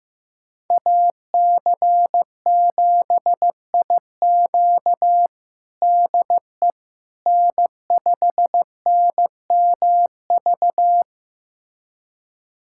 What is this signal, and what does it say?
Morse code, heard as a keyed tone, about 15 wpm, 700 Hz. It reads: AC7IQ DE N5NMV
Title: USA Callsign DE Callsign - 5 Characters - 15wpm